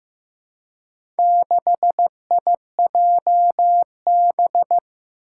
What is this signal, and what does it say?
Morse code, heard as a keyed tone, about 15 wpm, 700 Hz. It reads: 6IJB
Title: Four Challenging Letter-Number Combinations 15wpm